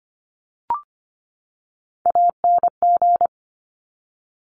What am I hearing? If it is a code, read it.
Morse code, heard as a keyed tone, about 25 wpm, 700 Hz. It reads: ANG
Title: Top Three Letter Combinations in Words 25wpm